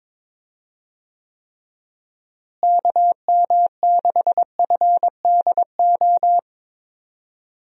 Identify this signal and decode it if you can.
Morse code, heard as a keyed tone, about 22 wpm, 700 Hz. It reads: KM6FDO
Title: USA Call Signs - 6 Characters - Repeated 2x - 22wpm